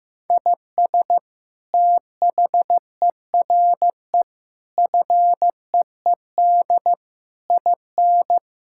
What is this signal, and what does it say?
Morse code, heard as a keyed tone, about 15 wpm, 700 Hz. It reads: IS THERE FEED IN
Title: Sentences from Top 1600 Words 15wpm